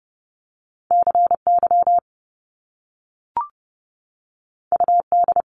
Morse code, heard as a keyed tone, about 30 wpm, 700 Hz. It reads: CY UD
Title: Top Two Letter Combinations in Words 30wpm